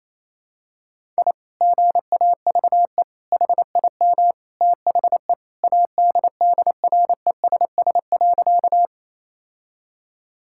Morse code, heard as a keyed tone, about 27 wpm, 700 Hz. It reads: I GAVE HIM THE ADDRESS.
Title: Sentences from Top 1000 Words - Review Part 6 - 28wpm